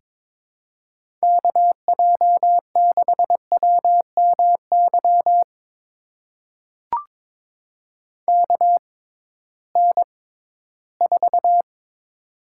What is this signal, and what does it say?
Morse code, heard as a keyed tone, about 22 wpm, spaced as for 14 wpm, 700 Hz. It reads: KJ6WMY KN4
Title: USA Call Signs - 6 Characters - 6x Character Spacing - 22wpm